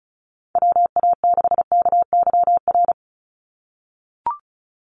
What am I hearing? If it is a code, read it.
Morse code, heard as a keyed tone, about 34 wpm, 700 Hz. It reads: WA6KYR
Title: NA SOTA Chaser Callsigns 35wpm